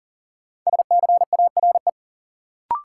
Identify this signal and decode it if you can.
Morse code, heard as a keyed tone, about 38 wpm, 700 Hz. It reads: SCARE
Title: Sentences from Top 1000 Words - Review Part 5 - New Word Drill - 40wpm